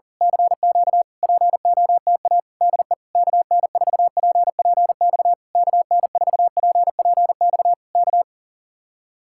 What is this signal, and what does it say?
Morse code, heard as a keyed tone, about 37 wpm, 700 Hz. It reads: CQ POTA DE KN4PPX KN4PPX K
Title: Speed-Racing Common QSO Phrases Encore - 40wpm